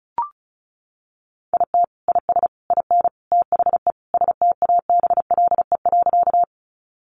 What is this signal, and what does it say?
Morse code, heard as a keyed tone, about 34 wpm, 700 Hz. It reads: IT IS IN THE STABLE.
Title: Sentences from Top 1500 Words 35wpm